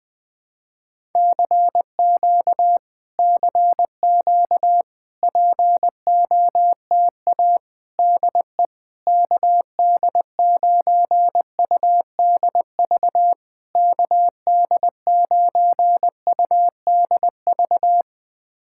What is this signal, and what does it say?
Morse code, heard as a keyed tone, about 20 wpm, 690 Hz. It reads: CQ CQ POTA DE KD9UDV KD9UDV
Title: POTA QSO Practice - 20wpm